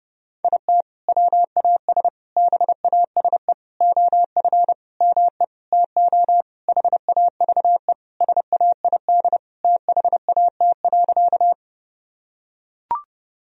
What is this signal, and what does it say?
Morse code, heard as a keyed tone, about 30 wpm, 700 Hz. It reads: IT WAS BASE OF ME TO HAVE SAID THAT.
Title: Sentences from Top 1000 Words - Review Part 1 - 30wpm